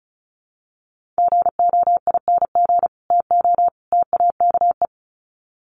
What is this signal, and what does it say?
Morse code, heard as a keyed tone, about 34 wpm, 700 Hz. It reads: GOING TO TAKE
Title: Sets of 3 Words 35wpm